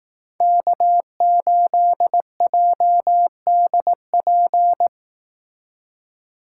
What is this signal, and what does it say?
Morse code, heard as a keyed tone, about 18 wpm, 700 Hz. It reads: K8JDP
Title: Speed Racing - USA Call Signs 15wpm